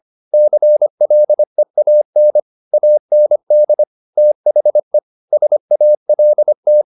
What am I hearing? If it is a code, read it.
Morse code, heard as a keyed tone, about 25 wpm, 600 Hz. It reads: CLEAN AND THE SALT